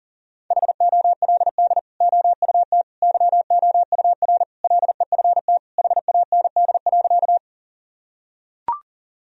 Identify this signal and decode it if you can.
Morse code, heard as a keyed tone, about 40 wpm, 700 Hz. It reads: HOLD OUT YOUR LEFT HAND.
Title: Sentences from Top 1000 Words - Review Part 2 - 40wpm